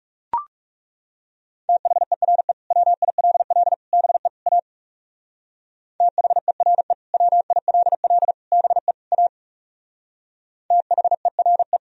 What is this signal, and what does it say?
Morse code, heard as a keyed tone, about 37 wpm, 700 Hz. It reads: THERE WILL BE A THERE WILL BE A THERE
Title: Speed-Racing - Sets of 4 Words 30wpm